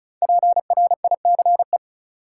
Morse code, heard as a keyed tone, about 34 wpm, 700 Hz. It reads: PRICE
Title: Sentences from Top 600 Words - New Word Drill - 35wpm